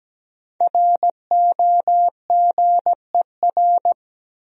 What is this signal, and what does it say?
Morse code, heard as a keyed tone, about 17 wpm, 700 Hz. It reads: ROGER
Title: USA Common Names - Max 5 Letters - 17wpm